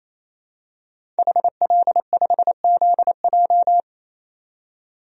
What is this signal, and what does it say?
Morse code, heard as a keyed tone, about 28 wpm, 700 Hz. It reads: HL5ZJ
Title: DXCC-100 Call Signs - 5 Characters - 28wpm